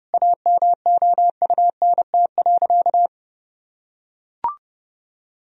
Morse code, heard as a keyed tone, about 30 wpm, 700 Hz. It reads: AMOUNT.
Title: Sentences from Top 1000 Words - Review Part 4 - 30wpm